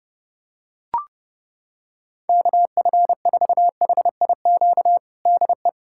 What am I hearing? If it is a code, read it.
Morse code, heard as a keyed tone, about 32 wpm, 700 Hz. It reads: KF4HIQ DE
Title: USA Callsign DE Callsign 30wpm